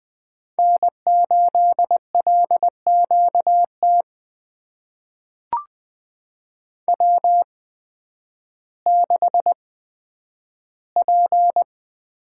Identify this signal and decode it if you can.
Morse code, heard as a keyed tone, about 20 wpm, spaced as for 11 wpm, 700 Hz. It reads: N8LQT W6P
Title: USA Call Signs - 5 Characters - 8x Character Spacing - 20wpm